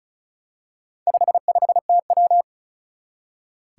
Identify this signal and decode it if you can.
Morse code, heard as a keyed tone, about 36 wpm, 700 Hz. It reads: 55TW
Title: Four Challenging Letter-Number Combinations 35wpm